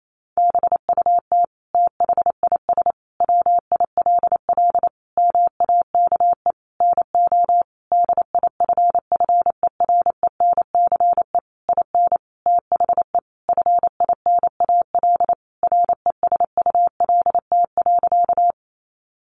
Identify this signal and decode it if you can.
Morse code, heard as a keyed tone, about 28 wpm, 700 Hz. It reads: BUT THIS WILL MAKE NO DIFFERENCE IN THE FINAL RESULT.
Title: Sentences from Top 800 Words 28wpm